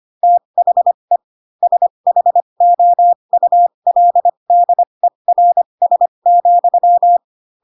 Morse code, heard as a keyed tone, about 25 wpm, spaced as for 22 wpm, 700 Hz. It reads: THE SHOULDERS,
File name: Pinocchio / Pinocchio ch3 18 25 ik1hge